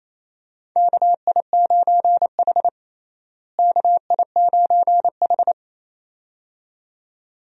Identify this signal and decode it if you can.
Morse code, heard as a keyed tone, about 28 wpm, 700 Hz. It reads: KI9H KI9H
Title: USA Call Signs - 4 Characters - Repeated 2x - 28wpm